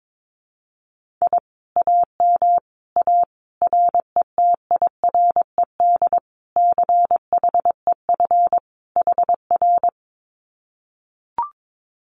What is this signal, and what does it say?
Morse code, heard as a keyed tone, about 22 wpm, 700 Hz. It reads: I AM A RETIRED CHEF HR
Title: Common QSO Phrases Encore 22wpm